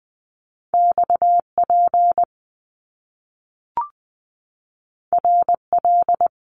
Morse code, heard as a keyed tone, about 20 wpm, 700 Hz. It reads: XP RL
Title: Top Two Letter Combinations in Words 20wpm